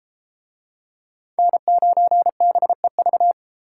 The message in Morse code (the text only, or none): N9BEV